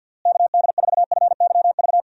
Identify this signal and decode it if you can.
Morse code, heard as a keyed tone, about 50 wpm, 700 Hz. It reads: KD4FYV
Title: USA Call Signs - 6 Characters - Repeated 3x - 50wpm